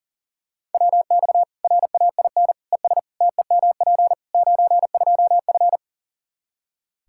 Morse code, heard as a keyed tone, about 40 wpm, 700 Hz. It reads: WX RAIN ES TEMP 92F
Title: Practice QSO - Exchange 2 - Info - 40wpm